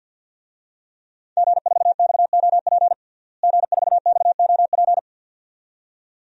Morse code, heard as a keyed tone, about 52 wpm, 700 Hz. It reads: G4XOP G4XOP
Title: DXCC-100 Call Signs - 5 Characters - Repeated 2x - 50wpm